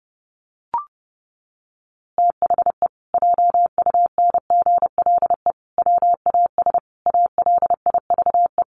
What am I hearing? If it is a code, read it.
Morse code, heard as a keyed tone, about 30 wpm, 700 Hz. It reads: THE JUNGLE WAS ALIVE
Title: A Sound Of Thunder 30wpm